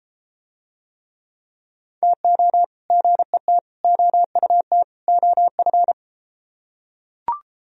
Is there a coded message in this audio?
Yes: Morse code from 2.0 to 5.9 s, heard as a keyed tone, about 32 wpm, 700 Hz. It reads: TO GET OUT OF